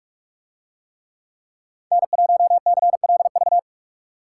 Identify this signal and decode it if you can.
Morse code, heard as a keyed tone, about 43 wpm, 700 Hz. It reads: N1CLV